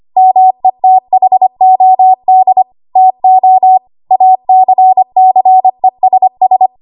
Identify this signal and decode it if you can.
Morse code, heard as a keyed tone, about 25 wpm, 750 Hz. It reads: METHOD TO ACCESS
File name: Archive / 25 WPM / 25WPM